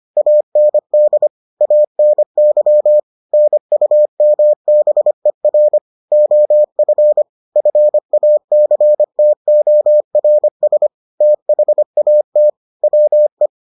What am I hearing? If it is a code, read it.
Morse code, heard as a keyed tone, about 25 wpm, 600 Hz. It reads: AND ANY NUMBER OF FACTORS THAT WE